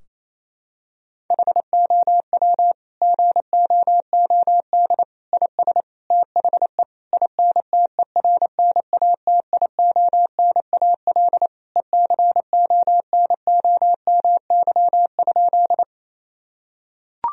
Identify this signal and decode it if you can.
Morse code, heard as a keyed tone, about 28 wpm, 700 Hz. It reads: HOW GOOD IS THE INTERNATIONAL ECONOMY?
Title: Sentences from Top 1000 Words - Review Part 4 - 28wpm